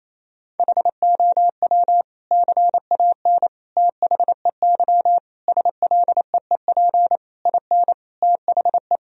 Morse code, heard as a keyed tone, about 28 wpm, 700 Hz. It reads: HOW CAN THEY SLEEP IN THE
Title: Sentences from Top 1000 Words - Review Part 3 - 28wpm